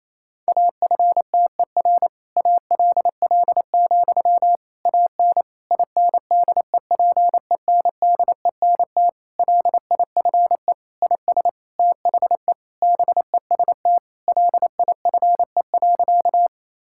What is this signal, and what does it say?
Morse code, heard as a keyed tone, about 27 wpm, 700 Hz. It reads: AFTER ALL, AN INDEPENDENT LIFE IS THE BEST LIFE.
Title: Sentences from Top 1000 Words - Review Part 6 - 28wpm